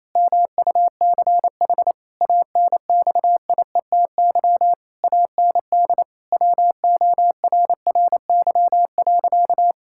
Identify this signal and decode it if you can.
Morse code, heard as a keyed tone, about 28 wpm, 700 Hz. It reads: MUCH ANXIETY AND WORRY.
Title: Pangram Challenge v2 - 28wpm